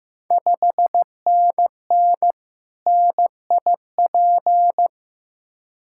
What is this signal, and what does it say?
Morse code, heard as a keyed tone, about 15 wpm, 700 Hz. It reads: 5NN NIP
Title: Ontario QSO Party Exchange 15wpm